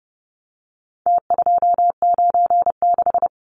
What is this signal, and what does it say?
Morse code, heard as a keyed tone, about 30 wpm, 700 Hz. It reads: T296
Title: Numbers - Contest Style - 4-Digit 30wpm